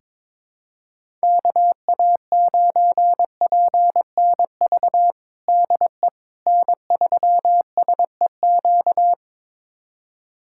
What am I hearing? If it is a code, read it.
Morse code, heard as a keyed tone, about 21 wpm, 700 Hz. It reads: KA9PNV DE N3SEQ
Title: USA Callsign DE Callsign 22wpm